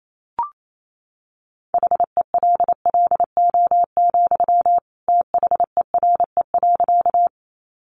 Morse code, heard as a keyed tone, about 28 wpm, 700 Hz. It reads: HELLO, THERE.
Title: Sentences from Top 1100 Words 28wpm